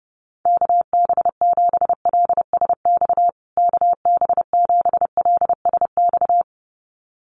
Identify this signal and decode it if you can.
Morse code, heard as a keyed tone, about 30 wpm, 700 Hz. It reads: KB7LSX KB7LSX
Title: USA Call Signs Repeated 2x 30wpm